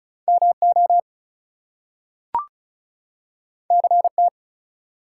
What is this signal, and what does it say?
Morse code, heard as a keyed tone, about 36 wpm, 700 Hz. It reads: MO CT
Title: US States and Canadian Provinces 35wpm